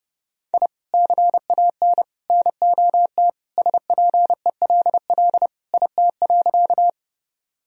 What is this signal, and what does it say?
Morse code, heard as a keyed tone, about 30 wpm, 700 Hz. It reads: I CAN NOT SPELL IT.
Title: Sentences from Top 1000 Words - Review Part 1 - 30wpm